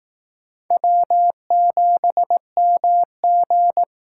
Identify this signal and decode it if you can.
Morse code, heard as a keyed tone, about 18 wpm, 700 Hz. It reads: W7MG
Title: Speed Racing - USA 1x2 Call Signs 15wpm